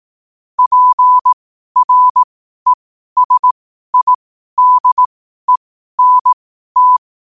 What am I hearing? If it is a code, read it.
Morse code, heard as a keyed tone, about 18 wpm, spaced as for 12 wpm, 1000 Hz. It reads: PRESIDENT
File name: qotdfiles / 12wpm